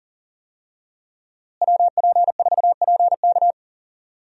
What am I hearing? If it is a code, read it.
Morse code, heard as a keyed tone, about 38 wpm, 700 Hz. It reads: WP4PK
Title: USA Call Signs - 5 Characters - Repeated 2x - 40wpm